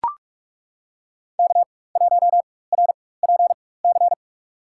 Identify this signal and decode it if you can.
Morse code, heard as a keyed tone, about 42 wpm, spaced as for 20 wpm, 700 Hz. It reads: K1RPC
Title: USA Call Signs - 4x Character Spacing - 45wpm